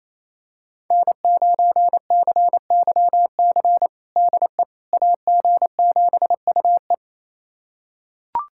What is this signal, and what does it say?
Morse code, heard as a keyed tone, about 27 wpm, 700 Hz. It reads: N9CYC DE AG7UE
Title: USA Callsign DE Callsign - 5 Characters - 28wpm